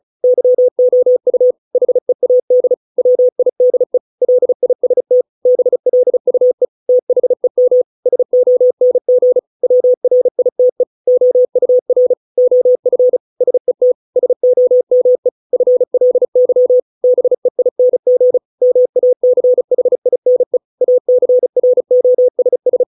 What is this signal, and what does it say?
Morse code, heard as a keyed tone, about 34 wpm, 500 Hz. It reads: YOU HEAD WIDE LIST BLUE THEM SONG WRITE OUR OF SET SOME FLY BEING MACHINE ACROSS